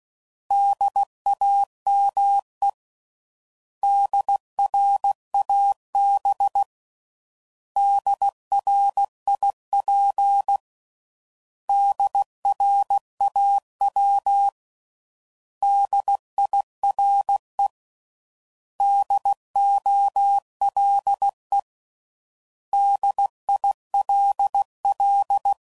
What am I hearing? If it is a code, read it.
Morse code, heard as a keyed tone, about 16 wpm, 800 Hz. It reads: DAME DRAB DRIP DRAW DIRE DOLE DILL
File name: Prefixes / By First Letter / Ds